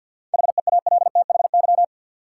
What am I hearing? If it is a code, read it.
Morse code, heard as a keyed tone, about 48 wpm, 700 Hz. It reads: HEALTHY
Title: Sentences from Top 1000 Words - Review Part 5 - New Word Drill - 50wpm